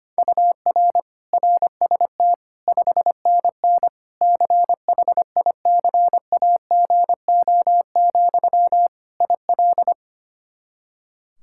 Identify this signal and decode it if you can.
Morse code, heard as a keyed tone, about 25 wpm, 700 Hz. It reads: UR RST 5NN CHICAGO, IL
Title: K1USN SST QSO Practice 25wpm